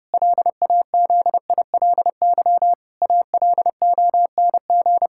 Morse code, heard as a keyed tone, about 30 wpm, 700 Hz. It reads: LAZILY ALONG
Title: Pangram Challenge v2 - 30wpm